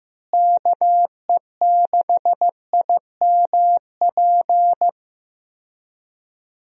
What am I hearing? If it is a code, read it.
Morse code, heard as a keyed tone, about 15 wpm, 700 Hz. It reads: KE6IMP
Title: USA Call Signs - 6 Characters - Repeated 3x - 15wpm